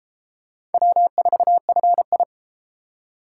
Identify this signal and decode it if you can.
Morse code, heard as a keyed tone, about 30 wpm, 700 Hz. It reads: W4FI